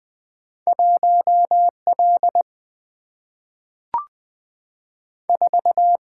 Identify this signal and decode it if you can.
Morse code, heard as a keyed tone, about 20 wpm, 700 Hz. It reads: 1 L 4
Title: Single-Letter Single-Number 20wpm